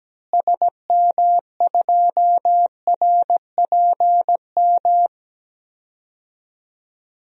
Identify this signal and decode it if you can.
Morse code, heard as a keyed tone, about 17 wpm, 700 Hz. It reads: SM2RPM
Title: DXCC-100 Call Signs - 6 Characters - Repeated 3x - 17wpm